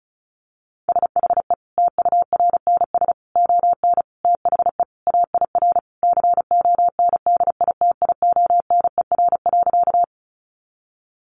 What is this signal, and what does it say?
Morse code, heard as a keyed tone, about 34 wpm, 700 Hz. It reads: SHE TURNS ON THE AIR CONDITIONER.
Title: Additional Easy Phrases 35wpm